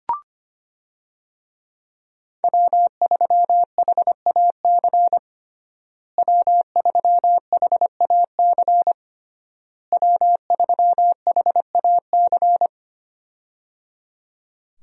Morse code, heard as a keyed tone, about 25 wpm, 700 Hz. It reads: W3HAC W3HAC W3HAC
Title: USA Call Signs - 5 Characters - Repeated 3x - 25wpm